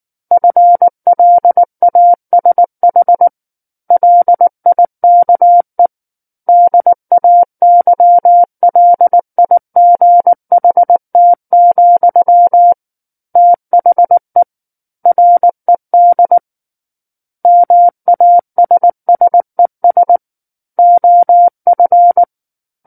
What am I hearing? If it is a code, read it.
Morse code, heard as a keyed tone, about 19 wpm, 700 Hz. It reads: FLASH LIKE DAYLIGHT, THE RED MASSES OF